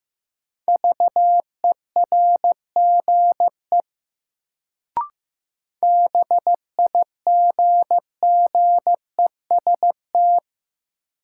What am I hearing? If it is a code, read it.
Morse code, heard as a keyed tone, about 15 wpm, 700 Hz. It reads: VERGE BIGGEST
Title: Sets of 4 Words - New Word Drill - 15wpm